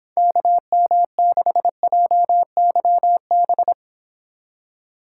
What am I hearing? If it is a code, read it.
Morse code, heard as a keyed tone, about 26 wpm, 700 Hz. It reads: KM6JYB